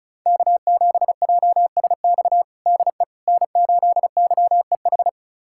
Morse code, heard as a keyed tone, about 36 wpm, 700 Hz. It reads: K7JSX DE N8YEH
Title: USA Callsign DE Callsign - 5 Characters - 35wpm